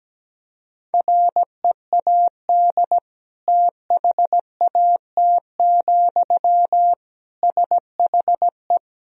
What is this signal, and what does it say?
Morse code, heard as a keyed tone, about 17 wpm, 700 Hz. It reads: READ THAT, SHE
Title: Sentences from Top 200 Words and Abbreviations 17wpm